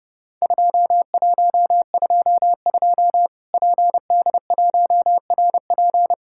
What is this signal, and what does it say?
Morse code, heard as a keyed tone, about 30 wpm, 700 Hz. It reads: 2122 PD1RP
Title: Worked All Europe DX Contest - QTC Exchange - 30wpm